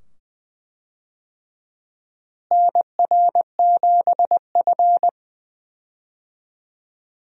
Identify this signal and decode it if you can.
Morse code, heard as a keyed tone, about 20 wpm, 700 Hz. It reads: NR7F
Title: USA Call Signs - 4 Characters - Repeated 2x - 20wpm